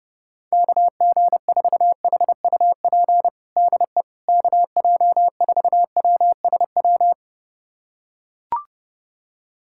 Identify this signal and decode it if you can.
Morse code, heard as a keyed tone, about 30 wpm, 700 Hz. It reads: KG4HUP DE KJ4WSW
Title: USA Callsign DE Callsign - 6 Characters - 30wpm